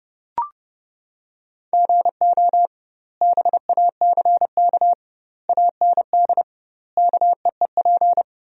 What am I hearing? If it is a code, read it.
Morse code, heard as a keyed tone, about 32 wpm, 700 Hz. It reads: GO BACK AND KEEP
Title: 2x Word-Spacing - Sentences from Top 300 Words - 30wpm